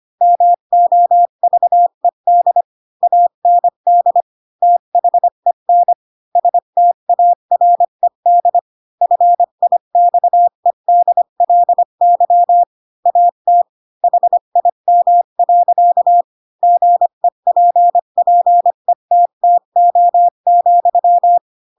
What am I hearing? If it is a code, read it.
Morse code, heard as a keyed tone, about 25 wpm, 700 Hz. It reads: MOVED AND THEN STARED FIXEDLY AT HIM. GEPPETTO,
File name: Pinocchio / Pinocchio ch3 20 25 ik1hge